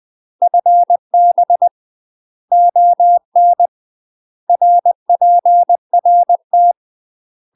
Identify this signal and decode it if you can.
Morse code, heard as a keyed tone, about 20 wpm, 700 Hz. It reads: FB ON RPRT
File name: UK MASTER QSOs TRAINING / 20WPM / UK MASTER QSOs 0024